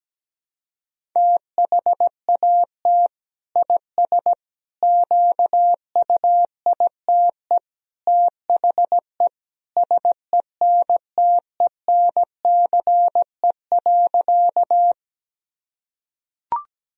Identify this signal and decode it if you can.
Morse code, heard as a keyed tone, about 17 wpm, 700 Hz. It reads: THAT IS QUITE THE SENTENCE.
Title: Sentences from Top 1000 Words - Review Part 1 - 17wpm